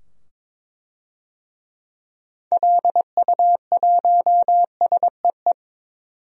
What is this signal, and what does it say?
Morse code, heard as a keyed tone, about 22 wpm, 700 Hz. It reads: LU1SEE
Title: DXCC-100 Call Signs - 6 Characters - 22wpm